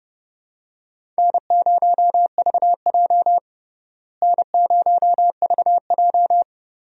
Morse code, heard as a keyed tone, about 32 wpm, 700 Hz. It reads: N0VJ N0VJ
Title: USA Call Signs - 4 Characters - Repeated 2x - 30wpm